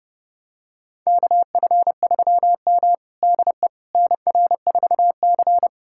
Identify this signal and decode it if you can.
Morse code, heard as a keyed tone, about 32 wpm, 700 Hz. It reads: KF3M DE NR4C
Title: USA Callsign DE Callsign - 4 Characters - 30wpm